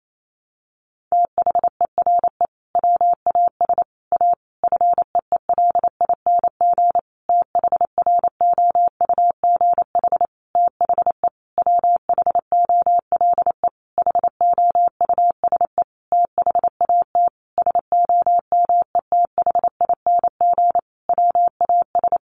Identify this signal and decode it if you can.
Morse code, heard as a keyed tone, about 28 wpm, 700 Hz. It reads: THERE WAS A FEELING THROUGH THE WHOLE HOUSE THAT SOMETHING WAS
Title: Sentences from Top 1200 Words 28wpm